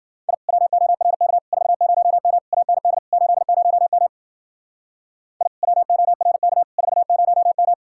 Morse code, heard as a keyed tone, about 48 wpm, 700 Hz. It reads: I WORK 40M AND 80M I WORK 40M
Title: Speed-Racing Common QSO Phrases - 40wpm